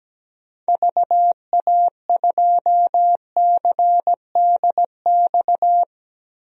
Morse code, heard as a keyed tone, about 17 wpm, 700 Hz. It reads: VA2CDX
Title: DXCC-100 Call Signs - 6 Characters - 17wpm